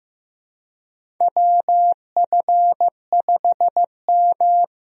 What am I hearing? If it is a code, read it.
Morse code, heard as a keyed tone, about 15 wpm, 700 Hz. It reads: WF5M
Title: USA Call Signs - 4 Characters - Repeated 3x - 15wpm